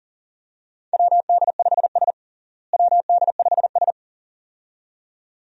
Morse code, heard as a keyed tone, about 40 wpm, 700 Hz. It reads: WD5S WD5S
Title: USA Call Signs - 4 Characters - Repeated 2x - 40wpm